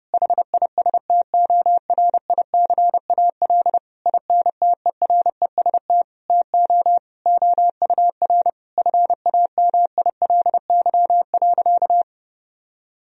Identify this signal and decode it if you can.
Morse code, heard as a keyed tone, about 30 wpm, 700 Hz. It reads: HISTORICAL INTEREST TO OUR FAMILY.
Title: Sentences from Top 1000 Words - Review Part 4 - 30wpm